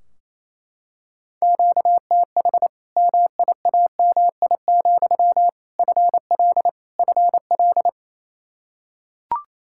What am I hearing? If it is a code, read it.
Morse code, heard as a keyed tone, about 28 wpm, 700 Hz. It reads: QTH MIAMI, FL FL
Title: Practice QSO - Exchange 1 - Beginning - 28wpm